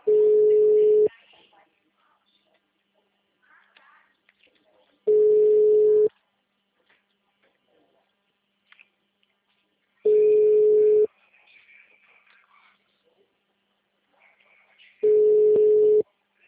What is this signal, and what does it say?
Ringback tone (Europe, 425 Hz).